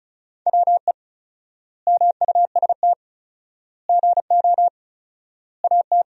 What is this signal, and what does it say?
Morse code, heard as a keyed tone, about 34 wpm, 700 Hz. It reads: WE MUST GO AT
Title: 4x Word-Spacing - Sentences from Top 300 Words - 35wpm